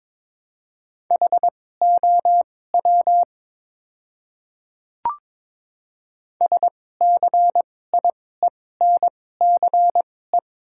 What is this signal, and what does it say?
Morse code, heard as a keyed tone, about 22 wpm, spaced as for 16 wpm, 700 Hz. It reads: HOW SCIENCE
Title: 2x Character Spacing - Top 500 Words - 22wpm